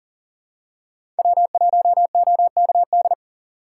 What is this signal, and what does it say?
Morse code, heard as a keyed tone, about 39 wpm, 700 Hz. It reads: W1OKD
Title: USA Call Signs - 5 Characters - 3x Character Spacing - 40wpm